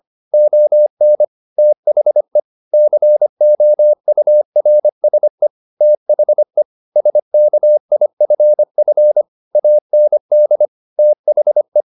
Morse code, heard as a keyed tone, about 25 wpm, 600 Hz. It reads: ON THE COURSE THE SKIFF AND THE